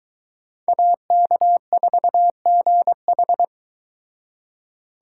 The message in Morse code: AK4GH

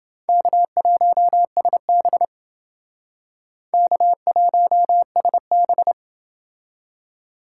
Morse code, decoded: K1SB K1SB

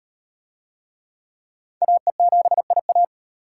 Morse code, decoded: AE7IA

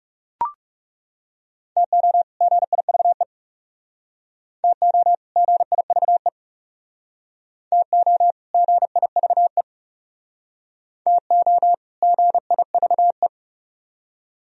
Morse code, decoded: TO GIVE TO GIVE TO GIVE TO GIVE

